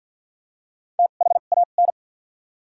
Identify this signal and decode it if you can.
Morse code, heard as a keyed tone, about 53 wpm, 700 Hz. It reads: T H A N
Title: 2x Character Spacing - Top 200 Words - 50wpm